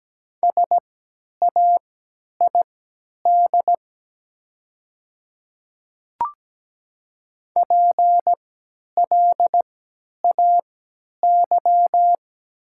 Morse code, heard as a keyed tone, about 17 wpm, spaced as for 10 wpm, 700 Hz. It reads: SAID PLAY
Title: 3x Character Spacing - Top 200 Words - 17wpm